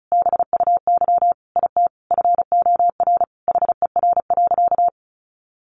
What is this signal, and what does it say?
Morse code, heard as a keyed tone, about 34 wpm, 700 Hz. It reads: BUY IT FOR HER.
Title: Sentences from Top 1400 Words 35wpm